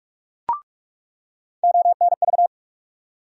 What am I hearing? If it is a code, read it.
Morse code, heard as a keyed tone, about 48 wpm, 700 Hz. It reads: ONV
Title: USA Call Sign Postfixes - 3 Letters 45wpm